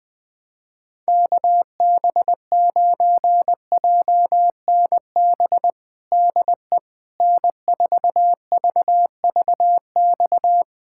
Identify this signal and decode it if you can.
Morse code, heard as a keyed tone, about 21 wpm, 700 Hz. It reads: KB9JNB DE N4VVX
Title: USA Callsign DE Callsign 20wpm